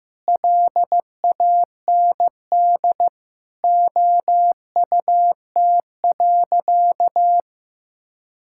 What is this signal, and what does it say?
Morse code, heard as a keyed tone, about 15 wpm, 700 Hz. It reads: LAND OUT.
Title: Sentences from Top 1000 Words - Review Part 5 - 15wpm